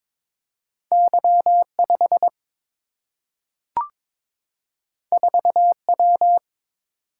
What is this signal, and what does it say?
Morse code, heard as a keyed tone, about 22 wpm, 700 Hz. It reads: Y 5 4 W
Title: Single-Letter Single-Number 22wpm